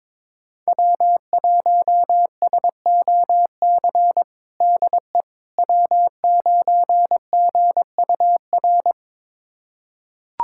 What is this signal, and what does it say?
Morse code, heard as a keyed tone, about 22 wpm, 700 Hz. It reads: W1SOC DE W9GUR
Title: USA Callsign DE Callsign - 5 Characters - 22wpm